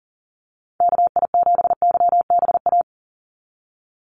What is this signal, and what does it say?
Morse code, heard as a keyed tone, about 40 wpm, 700 Hz. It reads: KI7YBA
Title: USA Call Signs 40wpm